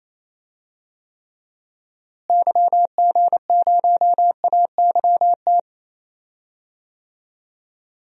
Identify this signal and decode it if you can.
Morse code, heard as a keyed tone, about 28 wpm, 700 Hz. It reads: YG0AYT
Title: DXCC-100 Call Signs - 6 Characters - Repeated 3x - 28wpm